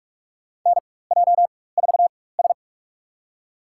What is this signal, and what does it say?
Morse code, heard as a keyed tone, about 47 wpm, spaced as for 21 wpm, 700 Hz. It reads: NJ4S
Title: USA Call Signs - 4 Characters - 4x Character Spacing - 45wpm